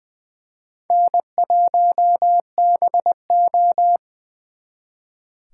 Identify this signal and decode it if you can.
Morse code, heard as a keyed tone, about 24 wpm, 700 Hz. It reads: N1BO
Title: Speed Racing - USA 1x2 Call Signs 20wpm